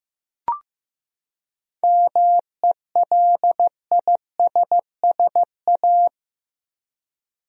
Morse code, heard as a keyed tone, about 15 wpm, 700 Hz. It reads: MELISSA
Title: USA Common Names - 15wpm